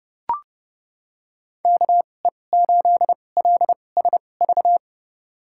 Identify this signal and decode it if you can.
Morse code, heard as a keyed tone, about 31 wpm, spaced as for 22 wpm, 700 Hz. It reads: KE8LSV
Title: USA Call Signs - 6 Characters - 2x Character Spacing - 30wpm